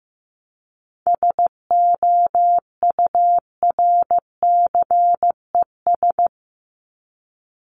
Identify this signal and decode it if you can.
Morse code, heard as a keyed tone, about 15 wpm, 700 Hz. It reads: SOURCES
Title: ES:S Suffix - Top 100 Words with Base Words 15wpm